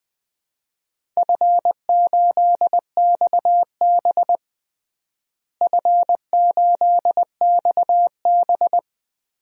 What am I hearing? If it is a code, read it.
Morse code, heard as a keyed tone, about 20 wpm, 700 Hz. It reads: F8XB F8XB
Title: DXCC-100 Call Signs - 4 Characters - Repeated 3x - 20wpm